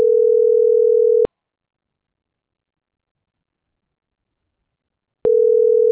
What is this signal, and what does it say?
Ringback tone (North America), 440 + 480 Hz.